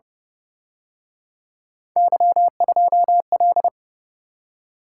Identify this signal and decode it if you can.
Morse code, heard as a keyed tone, about 29 wpm, 700 Hz. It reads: Y2L